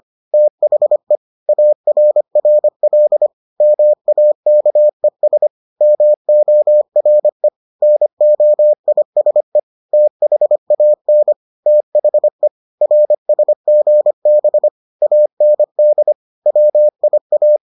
Morse code, heard as a keyed tone, about 25 wpm, 600 Hz. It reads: THE ARRL MAKES MORE NOISE THAN THE RSGB AND WIA